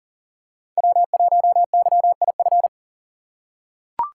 Morse code, heard as a keyed tone, about 40 wpm, 700 Hz. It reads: W1YIF
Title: USA Call Signs - 5 Characters - 2x Character Spacing - 40wpm